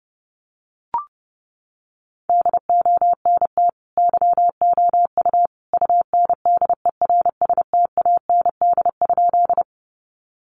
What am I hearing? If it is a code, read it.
Morse code, heard as a keyed tone, about 30 wpm, 700 Hz. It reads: DONT YOU UNDERSTAND?
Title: Sentences from Top 600 Words 30wpm